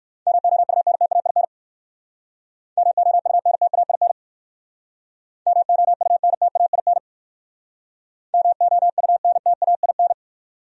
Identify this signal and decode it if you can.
Morse code, heard as a keyed tone, about 55 wpm, 700 Hz. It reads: MOUNTAIN MOUNTAIN MOUNTAIN MOUNTAIN
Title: Speed-Racing Top 300 Words 45wpm